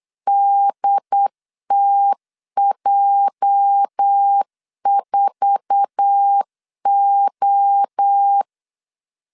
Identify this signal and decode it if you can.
Morse code, heard as a keyed tone, about 8 wpm, 800 Hz. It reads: DTJ4O
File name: LEZIONI / 9 2 8wpm 3437